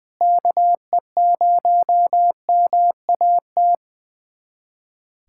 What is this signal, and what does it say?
Morse code, heard as a keyed tone, about 20 wpm, 700 Hz. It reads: KE0MAT
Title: USA Callsign DE Callsign - 6 Characters - 20wpm